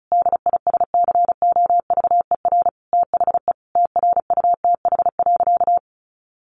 Morse code, heard as a keyed tone, about 34 wpm, 700 Hz. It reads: DISCOVER THE TRUTH.
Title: Sentences from Top 1100 Words 35wpm